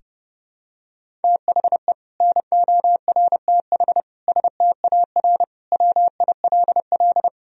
Morse code, heard as a keyed tone, about 32 wpm, 700 Hz. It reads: THE NORTH STAR WILL
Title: Sentences from Top 1000 Words - Review Part 3 - 30wpm